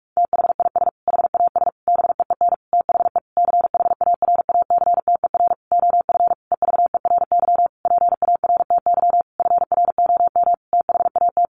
Morse code, heard as a keyed tone, about 44 wpm, 700 Hz. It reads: THIS HAS BEEN THE CHARACTER OF EVERY PARTY FROM THAT
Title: Sentences from Top 1400 Words 45wpm